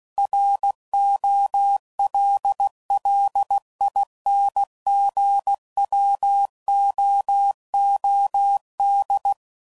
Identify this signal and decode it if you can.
Morse code, heard as a keyed tone, about 16 wpm, 800 Hz. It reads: ROLLINGWOOD
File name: CW Code Groups @15WPM / fromSM's / Cities